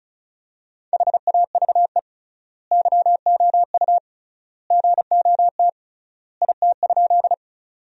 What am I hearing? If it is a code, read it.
Morse code, heard as a keyed tone, about 36 wpm, 700 Hz. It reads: HAVE YOU GOT IT?
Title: 3x Word-Spacing - Sentences from Top 300 Words - 35wpm